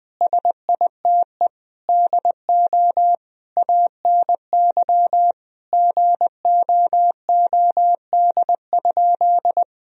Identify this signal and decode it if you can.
Morse code, heard as a keyed tone, about 20 wpm, 700 Hz. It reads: SITE DO ANY GOOD?
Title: Sentences from Top 1000 Words - Review Part 4 - 20wpm